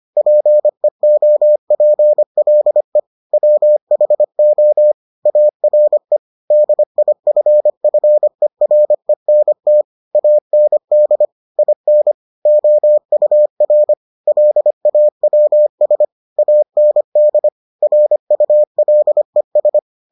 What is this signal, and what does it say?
Morse code, heard as a keyed tone, about 25 wpm, 600 Hz. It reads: PEOPLE WHO ARE DIFFERENT AND IN OUR LAWS AND RULES